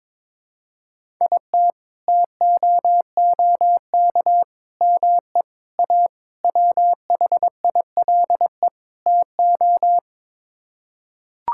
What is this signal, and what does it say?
Morse code, heard as a keyed tone, about 21 wpm, 700 Hz. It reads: IT TOOK ME A WHILE TO
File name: practice / 22wpm / Sets of 6 Words - 22wpm